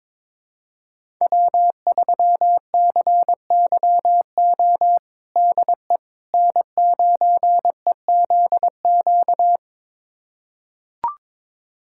Morse code, heard as a keyed tone, about 22 wpm, 700 Hz. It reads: W3CYO DE N9EZQ
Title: USA Callsign DE Callsign - 5 Characters - 22wpm